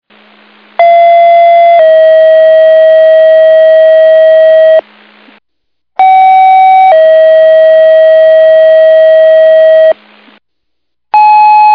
Two-tone page, heard as a one-second tone, then a three-second tone, about 680 then 640 Hz, about 760 then 640 Hz. It is heard from 0.8 to 9.9 s.
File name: Big Moose